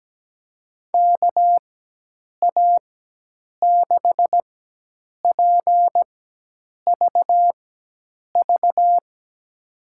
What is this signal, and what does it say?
Morse code, heard as a keyed tone, about 17 wpm, spaced as for 8 wpm, 700 Hz. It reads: KA6PVV
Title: USA Call Signs - 6 Characters - 4x Character Spacing - 17wpm